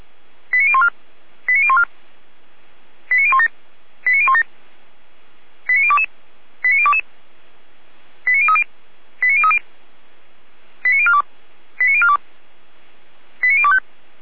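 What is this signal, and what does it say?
Five-tone selective call (ZVEI), heard as quick runs of five tones, tones of about 70 ms. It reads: 89014 89014 89017 89017 89022 89022 89030 89030 89042 89042 89025
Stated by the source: ZVEI five-tone selective call received by SDR.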